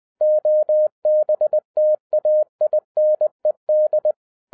Morse code, heard as a keyed tone, about 20 wpm, 610 Hz. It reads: OBTAINED
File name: 2012-01-mp3 / n05-CAME BACK TO HAM RADIO AFTER 30 YEARS OF INTERVAL-20wpm